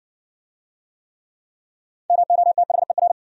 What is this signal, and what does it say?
Morse code, heard as a keyed tone, about 60 wpm, 700 Hz. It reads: MOTHER